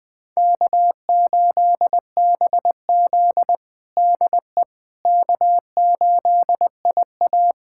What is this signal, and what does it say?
Morse code, heard as a keyed tone, about 21 wpm, 700 Hz. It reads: K8BZ DE K8IA
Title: K1USN SST QSO Practice 20wpm